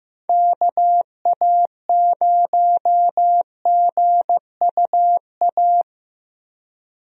Morse code, heard as a keyed tone, about 15 wpm, 700 Hz. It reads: KA0GUA
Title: USA Call Signs - 6 Characters - 15wpm